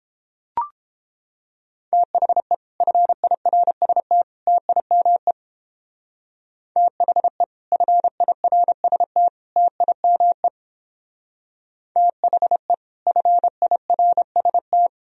Morse code, heard as a keyed tone, about 31 wpm, 700 Hz. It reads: THE FIRST TIME THE FIRST TIME THE FIRST